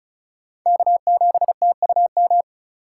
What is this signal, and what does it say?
Morse code, heard as a keyed tone, about 34 wpm, 700 Hz. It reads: K7TUM
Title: USA Call Signs - 5 Characters - 2x Character Spacing - 35wpm